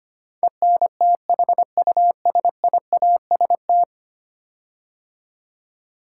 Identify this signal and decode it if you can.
Morse code, heard as a keyed tone, about 25 wpm, 700 Hz. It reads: ENTHUSIAST
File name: 10 Letter Words - First 3 Letters Given - 25wpm